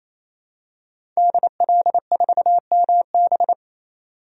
Morse code, heard as a keyed tone, about 28 wpm, 700 Hz. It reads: DL4MB
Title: DXCC-100 Call Signs - 5 Characters - 28wpm